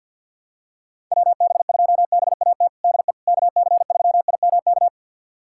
Morse code, heard as a keyed tone, about 51 wpm, 700 Hz. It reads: WB2BAT DE KC3IMK
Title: USA Callsign DE Callsign - 6 Characters - 50wpm